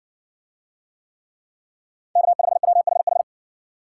Morse code, heard as a keyed tone, about 59 wpm, 700 Hz. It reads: K5WHH